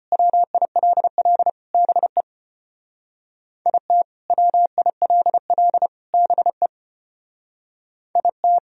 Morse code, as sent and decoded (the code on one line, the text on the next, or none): .-- .. .-.. .-.. / -... / .. - / .-- .. .-.. .-.. / -... / .. -
WILL BE IT WILL BE IT